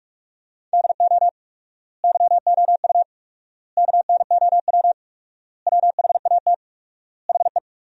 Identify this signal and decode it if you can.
Morse code, heard as a keyed tone, about 47 wpm, 700 Hz. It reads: DO YOU KNOW WHAT HE
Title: 4x Word-Spacing - Sentences from Top 300 Words - 45wpm